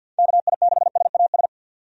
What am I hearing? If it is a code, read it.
Morse code, heard as a keyed tone, about 50 wpm, 700 Hz. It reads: KI6SAS
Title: USA Call Signs - 6 Characters - 50wpm